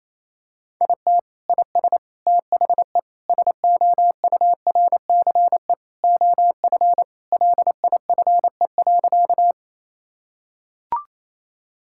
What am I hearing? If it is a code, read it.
Morse code, heard as a keyed tone, about 28 wpm, 700 Hz. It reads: IT IS THE SOURCE OF LIFE.
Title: Sentences from Top 1000 Words - Review Part 5 - 28wpm